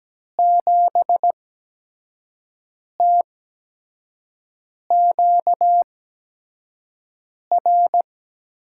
Morse code, heard as a keyed tone, about 17 wpm, spaced as for 5 wpm, 700 Hz. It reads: 7TQR
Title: USA Call Signs - 8x Character Spacing - 17wpm